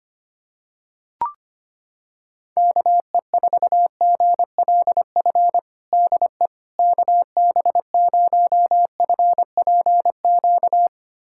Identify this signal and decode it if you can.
Morse code, heard as a keyed tone, about 25 wpm, 700 Hz. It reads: KE4GLF DE KB0FPQ
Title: USA Callsign DE Callsign 25wpm